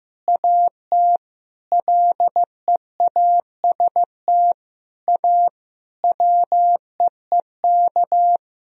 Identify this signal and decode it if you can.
Morse code, heard as a keyed tone, about 15 wpm, 700 Hz. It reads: AT LEAST A WEEK
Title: Sentences from Top 1000 Words - Review Part 2 - 15wpm